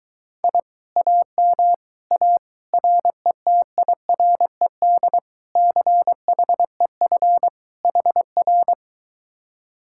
Morse code, heard as a keyed tone, about 22 wpm, 700 Hz. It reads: I AM A RETIRED CHEF HR